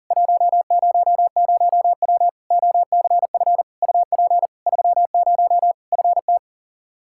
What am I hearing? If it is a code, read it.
Morse code, heard as a keyed tone, about 40 wpm, 700 Hz. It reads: 100W OCF UP 30 FT
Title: Practice QSO - Exchange 2 - Info - 40wpm